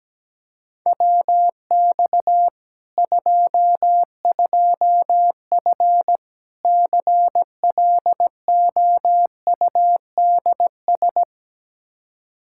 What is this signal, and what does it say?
Morse code, heard as a keyed tone, about 17 wpm, 700 Hz. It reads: WX 22F CLOUDS
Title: Practice QSO - Exchange 2 - Info - 17wpm